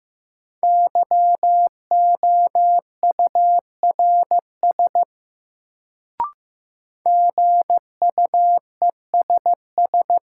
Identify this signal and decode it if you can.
Morse code, heard as a keyed tone, about 15 wpm, 700 Hz. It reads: YOURS GUESS
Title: Sentences from Top 1500 Words - New Word Drill Alt - 15wpm